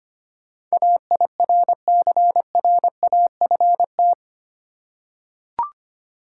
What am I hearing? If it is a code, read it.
Morse code, heard as a keyed tone, about 25 wpm, 700 Hz. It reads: AIRCRAFT
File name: Eight Letter Words - 25wpm